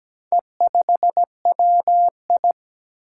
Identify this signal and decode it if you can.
Morse code, heard as a keyed tone, about 17 wpm, 700 Hz. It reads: E5WI
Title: Four Challenging Letter-Number Combinations 17wpm